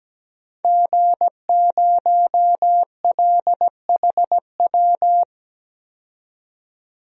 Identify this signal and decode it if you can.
Morse code, heard as a keyed tone, about 17 wpm, 700 Hz. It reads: G0LHW
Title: DXCC-100 Call Signs - 5 Characters - 17wpm